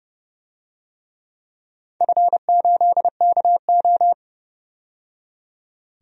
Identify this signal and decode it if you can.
Morse code, heard as a keyed tone, about 30 wpm, 700 Hz. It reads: F8KO
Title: DXCC-100 Call Signs - 4 Characters - 30wpm